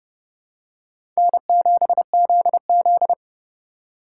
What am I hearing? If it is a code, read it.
Morse code, heard as a keyed tone, about 30 wpm, 700 Hz. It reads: N7ZZ
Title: USA Call Signs - 3x Character Spacing - 30wpm